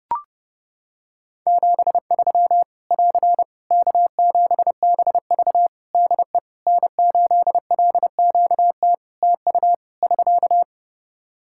Morse code, heard as a keyed tone, about 31 wpm, 700 Hz. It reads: 73 <AR> K7BV DE N8LQT TU <SK>
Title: Practice QSO - Exchange 4 - Ending - 30wpm